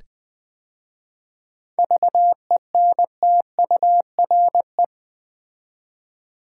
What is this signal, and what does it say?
Morse code, heard as a keyed tone, about 20 wpm, 700 Hz. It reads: VENTURE
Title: 7 Letter Words - First 3 Letters Given - 20wpm